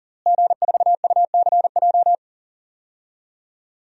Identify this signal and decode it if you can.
Morse code, heard as a keyed tone, about 40 wpm, 700 Hz. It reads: G4UCJ
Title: DXCC-100 Call Signs - 5 Characters - Repeated 2x - 40wpm